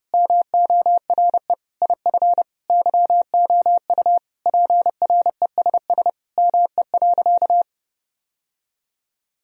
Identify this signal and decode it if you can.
Morse code, heard as a keyed tone, about 32 wpm, 700 Hz. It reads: MORE IF YOU PRESS ME.
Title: Sentences from Top 1000 Words - Review Part 1 - 30wpm